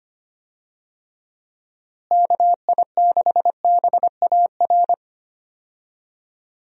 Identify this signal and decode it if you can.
Morse code, heard as a keyed tone, about 25 wpm, 700 Hz. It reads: KI6BAR
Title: USA Call Signs - 6 Characters - Repeated 2x - 25wpm